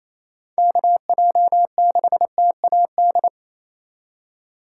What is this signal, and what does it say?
Morse code, heard as a keyed tone, about 28 wpm, 700 Hz. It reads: KJ6TAD
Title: USA Call Signs - 3x Character Spacing - 28wpm